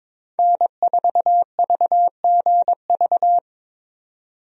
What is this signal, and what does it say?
Morse code, heard as a keyed tone, about 22 wpm, 700 Hz. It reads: N4VGV
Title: USA Call Signs - 5 Characters - 22wpm